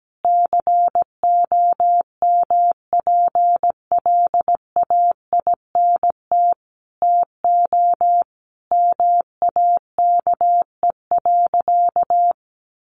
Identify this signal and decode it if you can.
Morse code, heard as a keyed tone, about 17 wpm, 700 Hz. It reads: COMPLAINT TO MAKE.
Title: Sentences from Top 1400 Words 17wpm